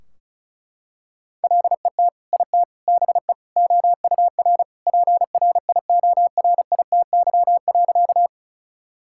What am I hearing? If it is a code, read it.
Morse code, heard as a keyed tone, about 36 wpm, 700 Hz. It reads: LET IT BE OUR PRIORITY.
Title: Sentences from Top 1600 Words 35wpm